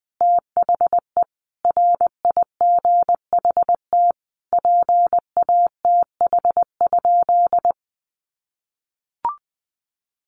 Morse code, heard as a keyed tone, about 20 wpm, 700 Hz. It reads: THE RIGHT PATH?
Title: Sentences from Top 1400 Words 20wpm